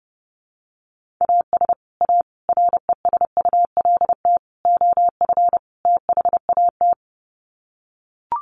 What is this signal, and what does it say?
Morse code, heard as a keyed tone, about 28 wpm, 700 Hz. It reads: AS A RESULT OF THAT
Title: Sets of 5 Words Encore 30wpm